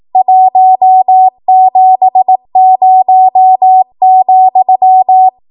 Morse code, heard as a keyed tone, about 18 wpm, 750 Hz. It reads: 170,
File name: Morse / 18WPM